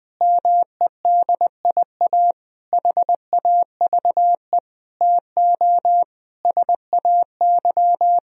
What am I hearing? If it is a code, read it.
Morse code, heard as a keyed tone, about 20 wpm, 700 Hz. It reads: MEDIA HAVE TO SAY
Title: Sentences from Top 1000 Words - Review Part 4 - 20wpm